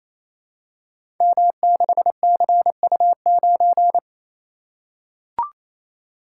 Morse code, heard as a keyed tone, about 28 wpm, 700 Hz. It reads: M6CU9
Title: Code Groups - 5 Letter:Numbers - 28wpm